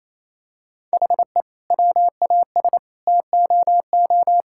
Morse code, heard as a keyed tone, about 28 wpm, 700 Hz. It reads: HE WAS TOO
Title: Sentences from Top 1000 Words - Review Part 6 - 28wpm